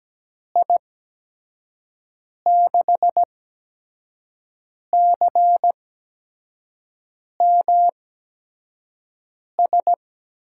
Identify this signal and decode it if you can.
Morse code, heard as a keyed tone, about 17 wpm, spaced as for 5 wpm, 700 Hz. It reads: I6CMS
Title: USA Call Signs - 6 Characters - 8x Character Spacing - 17wpm